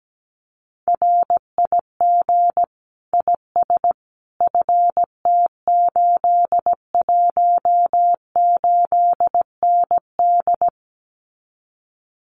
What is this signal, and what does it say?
Morse code, heard as a keyed tone, about 17 wpm, 700 Hz. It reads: RIG IS FT818ND
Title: Common QSO Phrases Encore 17wpm